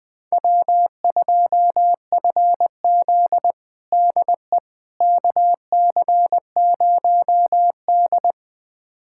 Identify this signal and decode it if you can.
Morse code, heard as a keyed tone, about 20 wpm, 700 Hz. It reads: W2FZ DE KC0D
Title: USA Callsign DE Callsign 20wpm